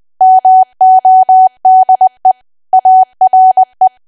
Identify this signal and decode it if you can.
Morse code, heard as a keyed tone, about 20 wpm, 750 Hz. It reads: MODE ARE
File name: cwa / W1AW-Level3 / 20WPM